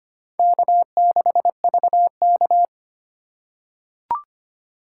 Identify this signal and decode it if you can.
Morse code, heard as a keyed tone, about 25 wpm, 700 Hz. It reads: K6VK
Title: USA Call Signs - 4 Characters - 2x Character Spacing - 25wpm